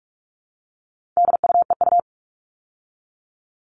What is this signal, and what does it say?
Morse code, heard as a keyed tone, about 44 wpm, 700 Hz. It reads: DUEU
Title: All Cut Numbers - 4-Digit 45wpm